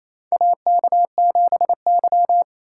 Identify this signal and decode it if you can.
Morse code, heard as a keyed tone, about 28 wpm, 700 Hz. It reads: AK7Y
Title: USA Call Signs - 4 Characters - 3x Character Spacing - 28wpm